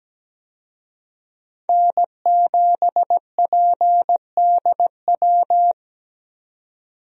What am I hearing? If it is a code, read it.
Morse code, heard as a keyed tone, about 17 wpm, 700 Hz. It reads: N7PDW
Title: USA Call Signs - 5 Characters - 17wpm